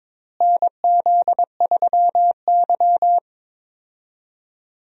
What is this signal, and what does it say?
Morse code, heard as a keyed tone, about 21 wpm, 700 Hz. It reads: NZ3Y
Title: USA Call Signs - 4 Characters - Repeated 2x - 22wpm